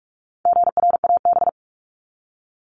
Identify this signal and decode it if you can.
Morse code, heard as a keyed tone, about 44 wpm, 700 Hz. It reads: GRAB
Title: Top 1751-2000 Words 45wpm